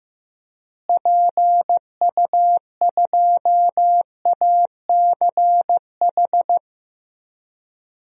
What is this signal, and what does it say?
Morse code, heard as a keyed tone, about 16 wpm, 700 Hz. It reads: PU2ACH
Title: DXCC-100 Call Signs - 6 Characters - Repeated 3x - 15wpm